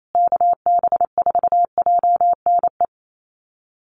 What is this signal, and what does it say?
Morse code, heard as a keyed tone, about 28 wpm, 700 Hz. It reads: KB4JNE
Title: USA Call Signs Repeated 3x 28wpm